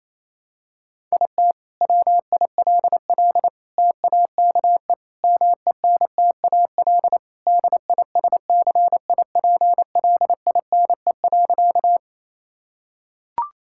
Morse code, heard as a keyed tone, about 28 wpm, 700 Hz. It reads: IT WILL TAKE MENTAL DISCIPLINE.
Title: Sentences from Top 1000 Words - Review Part 6 - 28wpm